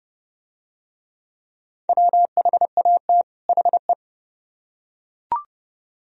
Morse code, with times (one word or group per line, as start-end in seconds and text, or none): 1.9-3.2 WHAT
3.5-3.9 HE